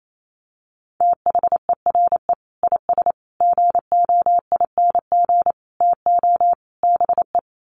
Morse code, heard as a keyed tone, about 28 wpm, 700 Hz. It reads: THERE IS GOING TO BE
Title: Sets of 5 Words Encore 28wpm